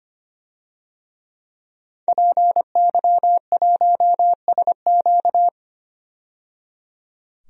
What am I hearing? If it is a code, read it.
Morse code, heard as a keyed tone, about 25 wpm, 700 Hz. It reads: PY1SQ
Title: DXCC-100 Call Signs - 5 Characters - 25wpm